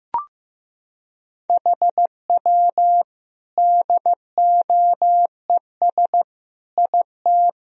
Morse code, heard as a keyed tone, about 15 wpm, 700 Hz. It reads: HW DOES IT
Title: Sentences from Top 400 Words and Abbreviations 15wpm